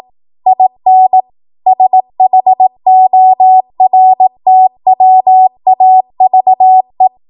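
Morse code, heard as a keyed tone, about 18 wpm, 750 Hz. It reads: IN SHORTWAVE
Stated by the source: ARRL 18 wpm Morse code practice archive transmission.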